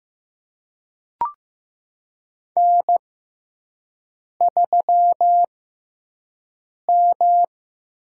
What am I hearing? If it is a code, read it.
Morse code, heard as a keyed tone, about 15 wpm, spaced as for 5 wpm, 700 Hz. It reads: N3M
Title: USA Call Signs - 6x Character Spacing - 15wpm